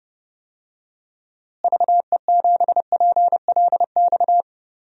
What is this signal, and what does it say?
Morse code, heard as a keyed tone, about 32 wpm, 700 Hz. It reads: VE7PLX
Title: DXCC-100 Call Signs - 6 Characters - Repeated 2x - 30wpm